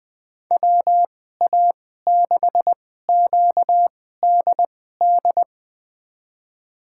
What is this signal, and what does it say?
Morse code, heard as a keyed tone, about 20 wpm, spaced as for 14 wpm, 700 Hz. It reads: WA6QDD
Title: USA Call Signs - 6 Characters - 2x Character Spacing - 20wpm